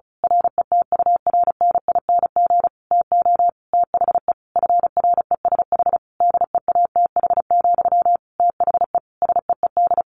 Morse code, heard as a keyed tone, about 34 wpm, 700 Hz. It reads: RETURNING TO THE FRESH DEATH, THE SEED
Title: A Sound Of Thunder 35wpm